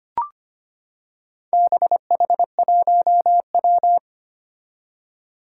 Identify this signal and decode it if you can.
Morse code, heard as a keyed tone, about 25 wpm, 700 Hz. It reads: BH1W
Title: Four Challenging Letter-Number Combinations 25wpm